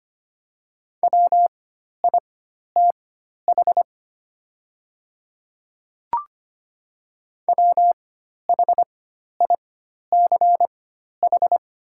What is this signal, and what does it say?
Morse code, heard as a keyed tone, about 25 wpm, spaced as for 12 wpm, 700 Hz. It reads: WITH WHICH
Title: 4x Character Spacing - Top 100 Words - 25wpm